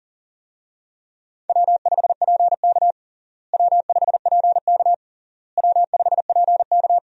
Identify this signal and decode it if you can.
Morse code, heard as a keyed tone, about 40 wpm, 700 Hz. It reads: W5PK W5PK W5PK
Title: USA Call Signs - 4 Characters - Repeated 3x - 40wpm